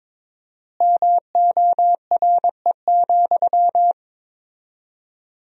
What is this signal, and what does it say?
Morse code, heard as a keyed tone, about 22 wpm, 700 Hz. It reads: MORE,
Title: 4x Word-Spacing - Sentences from Top 300 Words - 22wpm